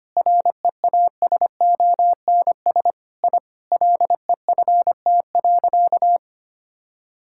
Morse code, heard as a keyed tone, about 25 wpm, 700 Hz. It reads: REASONS I LEFT.
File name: practice / 25wpm / Sentences with 4-Word Phrases - 25wpm